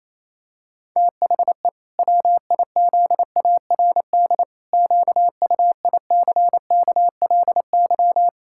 Morse code, heard as a keyed tone, about 28 wpm, 700 Hz. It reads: THE WIZARD QUICKLY
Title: Pangram Challenge v2 - 28wpm